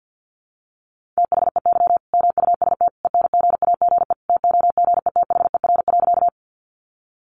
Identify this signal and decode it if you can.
Morse code, heard as a keyed tone, about 50 wpm, 700 Hz. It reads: THEY MUST ENGAGE TOGETHER.
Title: Sentences from Top 1500 Words 50wpm